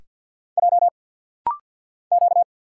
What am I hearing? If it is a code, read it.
Morse code, heard as a keyed tone, about 47 wpm, 700 Hz. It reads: J Q
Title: Single Letters - Rapid-Fire 50wpm